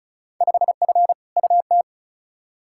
Morse code, heard as a keyed tone, about 36 wpm, 700 Hz. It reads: 5F UT
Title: Field Day Exchanges - 35wpm